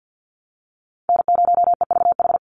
Morse code, heard as a keyed tone, about 50 wpm, 700 Hz. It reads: N0EVH
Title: NA SOTA Chaser Callsigns 50wpm